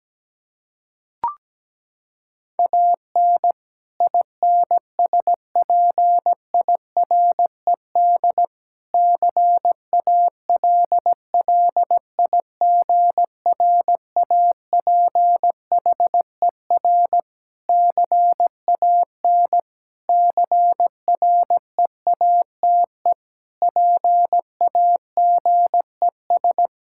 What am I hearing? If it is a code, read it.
Morse code, heard as a keyed tone, about 17 wpm, 700 Hz. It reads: AN INSPIRED CALLIGRAPHER CAN CREATE PAGES
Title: Pangram Challenge v2 - 17wpm